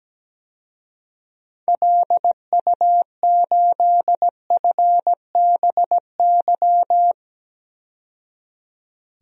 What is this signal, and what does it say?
Morse code, heard as a keyed tone, about 17 wpm, 700 Hz. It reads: LU8FBY
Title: DXCC-100 Call Signs - 6 Characters - 17wpm